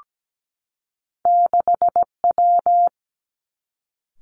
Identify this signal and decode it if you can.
Morse code, heard as a keyed tone, about 17 wpm, 700 Hz. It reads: 6W
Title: Two Challenging Letter-Number Combinations 17wpm